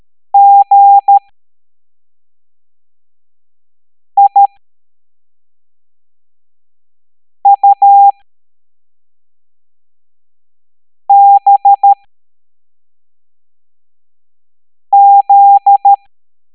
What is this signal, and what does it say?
Morse code, heard as a keyed tone, about 13 wpm, spaced as for 3 wpm, 800 Hz. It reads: GIUBZ